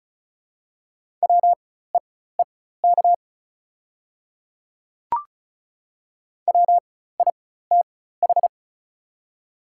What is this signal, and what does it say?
Morse code, heard as a keyed tone, about 34 wpm, spaced as for 16 wpm, 700 Hz. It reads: WEEK WITH
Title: 4x Character Spacing - Top 500 Words - 35wpm